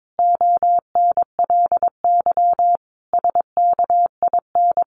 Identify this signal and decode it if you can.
Morse code, heard as a keyed tone, about 22 wpm, 700 Hz. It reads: ONLY SKIN
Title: Sentences from Top 1300 Words 22wpm